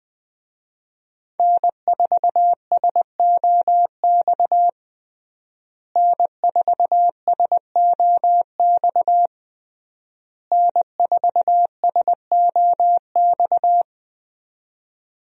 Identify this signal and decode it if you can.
Morse code, heard as a keyed tone, about 20 wpm, 700 Hz. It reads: N4SOX N4SOX N4SOX
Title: USA Call Signs - 5 Characters - Repeated 3x - 20wpm